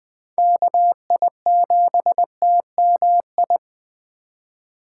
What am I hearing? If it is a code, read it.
Morse code, heard as a keyed tone, about 20 wpm, 700 Hz. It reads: KI7TMI
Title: USA Call Signs - 6 Characters - 4x Character Spacing - 20wpm